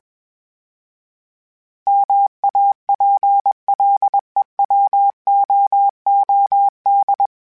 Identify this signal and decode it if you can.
Morse code, heard as a keyed tone, about 21 wpm, 800 Hz. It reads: MAPLEWOOD
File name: fromSMs / FromSMs mp3s / Cities@20